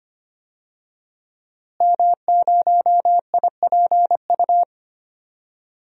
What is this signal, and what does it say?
Morse code, heard as a keyed tone, about 25 wpm, 700 Hz. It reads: M0IPU
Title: DXCC-100 Call Signs - 5 Characters - Repeated 2x - 25wpm